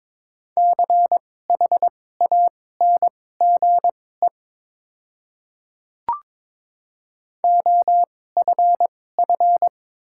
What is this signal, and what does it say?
Morse code, heard as a keyed tone, about 22 wpm, spaced as for 16 wpm, 700 Hz. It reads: CHANGE OFF
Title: 2x Character Spacing - Top 300 Words - 22wpm